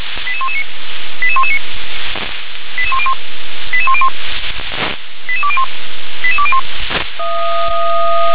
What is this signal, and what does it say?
Five-tone selective call (ZVEI), heard as quick runs of five tones, tones of about 70 ms. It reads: 88119 88119 88101 88101 88201 88201